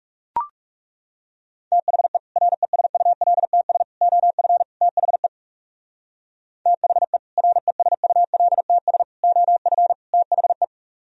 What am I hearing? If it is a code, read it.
Morse code, heard as a keyed tone, about 38 wpm, 700 Hz. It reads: THE RESULTS OF THE THE RESULTS OF THE